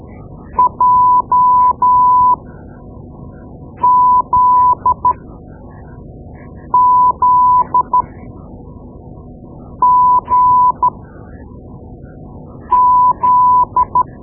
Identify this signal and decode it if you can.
Morse code, heard as a keyed tone, about 13 wpm, spaced as for 4 wpm, 1000 Hz. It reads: JZZGZ
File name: Morse Lessons / Morse Lessons / Tape-2B